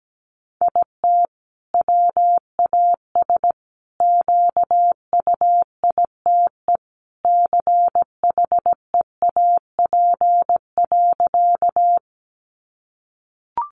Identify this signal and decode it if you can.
Morse code, heard as a keyed tone, about 17 wpm, 700 Hz. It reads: IT WAS QUITE CHEAP.
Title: Sentences from Top 1200 Words 17wpm